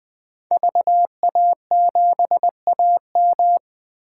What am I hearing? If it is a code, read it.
Morse code, heard as a keyed tone, about 21 wpm, 700 Hz. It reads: VA7AM
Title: K1USN SST QSO Practice - 20wpm